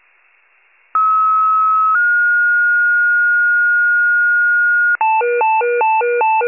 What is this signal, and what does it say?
Two-tone page, heard as a one-second tone, then a three-second tone, about 1290 then 1430 Hz. It is heard from 0.9 to 5.0 s.
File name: wcalert